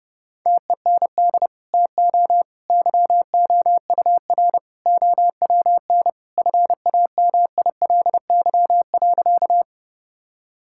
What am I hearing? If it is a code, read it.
Morse code, heard as a keyed tone, about 30 wpm, 700 Hz. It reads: TEND TO YOUR OWN FAMILY.
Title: Sentences from Top 1000 Words - Review Part 5 - 30wpm